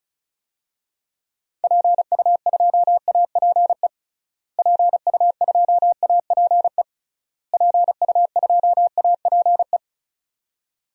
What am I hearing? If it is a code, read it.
Morse code, heard as a keyed tone, about 36 wpm, 700 Hz. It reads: PU2APE PU2APE PU2APE
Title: DXCC-100 Call Signs - 6 Characters - Repeated 3x - 35wpm